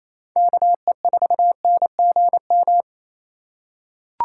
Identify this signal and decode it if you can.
Morse code, heard as a keyed tone, about 28 wpm, 700 Hz. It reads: KE4NGM
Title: USA Call Signs - 6 Characters - 8x Character Spacing - 28wpm